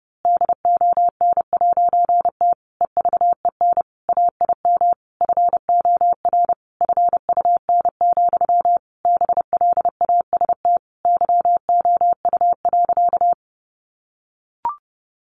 Morse code, heard as a keyed tone, about 30 wpm, 700 Hz. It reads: DON'T EVEN AIM FOR FUN, BLAST YOU.
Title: A Sound Of Thunder 30wpm